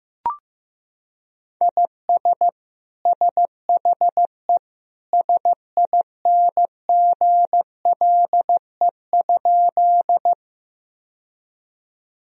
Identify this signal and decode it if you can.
Morse code, heard as a keyed tone, about 16 wpm, 700 Hz. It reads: IS SHE SINGLE?
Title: Sentences from Top 1000 Words - Review Part 4 - 15wpm